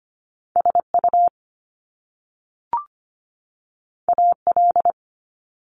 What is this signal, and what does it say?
Morse code, heard as a keyed tone, about 25 wpm, 700 Hz. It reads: SU AL
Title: Top Two Letter Combinations in Words 25wpm